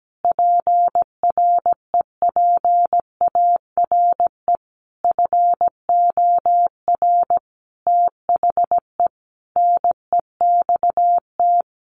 Morse code, heard as a keyed tone, about 17 wpm, 700 Hz. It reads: PREPARE FOR THE NEXT
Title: Sentences from Top 1200 Words 17wpm